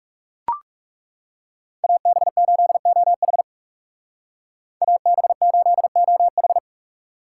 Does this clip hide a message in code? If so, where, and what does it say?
Yes: Morse code from 1.8 to 6.6 s, heard as a keyed tone, about 39 wpm, 700 Hz. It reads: AB8OH AB8OH